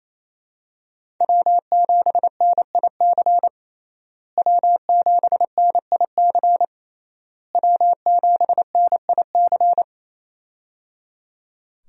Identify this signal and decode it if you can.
Morse code, heard as a keyed tone, about 27 wpm, 700 Hz. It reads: W7NIC W7NIC W7NIC
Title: USA Call Signs - 5 Characters - Repeated 3x - 28wpm